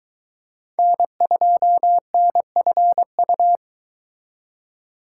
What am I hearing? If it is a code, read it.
Morse code, heard as a keyed tone, about 23 wpm, 700 Hz. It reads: N2NFU